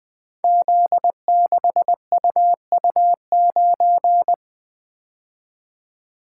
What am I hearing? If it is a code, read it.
Morse code, heard as a keyed tone, about 20 wpm, 700 Hz. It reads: Z6UU9
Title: Code Groups - 5 Letter:Numbers - 20wpm